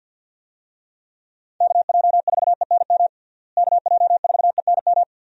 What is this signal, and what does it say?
Morse code, heard as a keyed tone, about 48 wpm, 700 Hz. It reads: KJ4ENM KJ4ENM
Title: USA Call Signs - 6 Characters - Repeated 2x - 50wpm